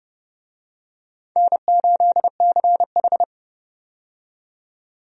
Morse code, heard as a keyed tone, about 30 wpm, 700 Hz. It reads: N8CH
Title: USA Call Signs - 4 Characters - 30wpm